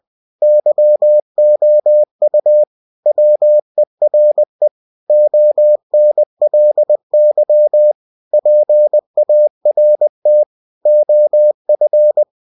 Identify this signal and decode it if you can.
Morse code, heard as a keyed tone, about 20 wpm, 600 Hz. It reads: YOU WERE ONLY PART OF
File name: cw-books / winnie-the-pooh / winnie-20wpm-page-0005